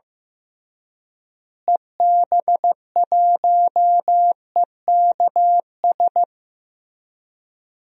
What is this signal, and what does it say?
Morse code, heard as a keyed tone, about 15 wpm, 700 Hz. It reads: EB1EKS
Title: DXCC-100 Call Signs - 6 Characters - Repeated 2x - 15wpm